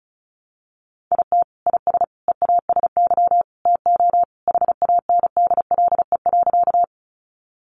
Morse code, heard as a keyed tone, about 34 wpm, 700 Hz. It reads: IT IS EASY TO HANDLE.
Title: Sentences from Top 1100 Words 35wpm